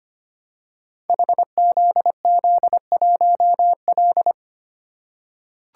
Morse code, heard as a keyed tone, about 25 wpm, 700 Hz. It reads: HZZ1L
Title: Code Groups - 5 Letter:Numbers - 25wpm